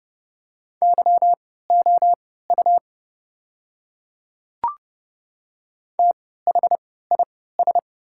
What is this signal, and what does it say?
Morse code, heard as a keyed tone, about 30 wpm, spaced as for 17 wpm, 700 Hz. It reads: YOU THIS
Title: 3x Character Spacing - Top 100 Words - 30wpm